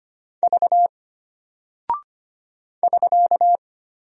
Morse code, heard as a keyed tone, about 25 wpm, 700 Hz. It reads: V <SK>
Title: Single Character (Letter:Number:Punctuation:Prosign) Alt 25wpm